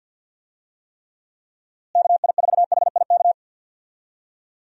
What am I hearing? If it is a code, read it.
Morse code, heard as a keyed tone, about 50 wpm, 700 Hz. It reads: KI4HIK
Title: USA Call Signs - 6 Characters - Repeated 3x - 50wpm